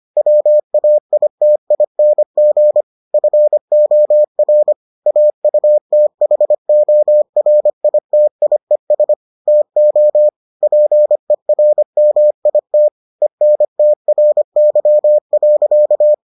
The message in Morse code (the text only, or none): WAITING FOR AUTHORITIES TO PERMIT ENTRY.